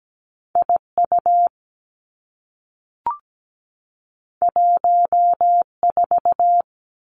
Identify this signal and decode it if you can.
Morse code, heard as a keyed tone, about 17 wpm, 700 Hz. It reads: IU 14
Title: Two Challenging Letter-Number Combinations 17wpm